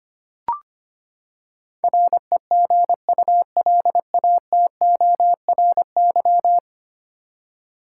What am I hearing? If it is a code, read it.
Morse code, heard as a keyed tone, about 25 wpm, 700 Hz. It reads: REGULATORY
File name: practice / 25wpm / Ten Letter Words - 25wpm